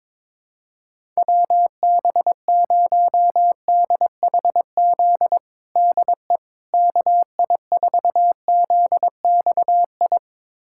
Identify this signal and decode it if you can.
Morse code, heard as a keyed tone, about 22 wpm, 700 Hz. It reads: WB0DHZ DE KI4ZXI
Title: USA Callsign DE Callsign - 6 Characters - 22wpm